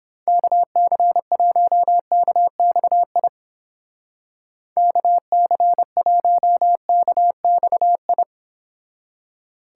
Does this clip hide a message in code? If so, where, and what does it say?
Yes: Morse code from 0.3 to 8.2 s, heard as a keyed tone, about 31 wpm, 700 Hz. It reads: KC1KXI KC1KXI